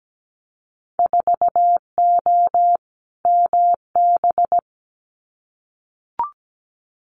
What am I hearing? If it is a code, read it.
Morse code, heard as a keyed tone, about 17 wpm, 700 Hz. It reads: 4O MB
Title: Winter Field Day Exchanges Encore 17wpm